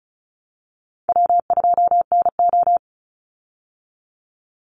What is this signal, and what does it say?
Morse code, heard as a keyed tone, about 34 wpm, 700 Hz. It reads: W2NO
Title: Easy Call Signs Repeated 2x - 35wpm